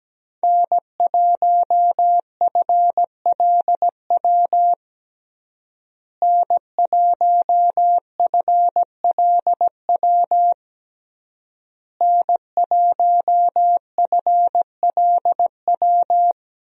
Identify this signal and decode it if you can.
Morse code, heard as a keyed tone, about 17 wpm, 700 Hz. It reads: N1FLW N1FLW N1FLW
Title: USA Call Signs - 5 Characters - Repeated 3x - 17wpm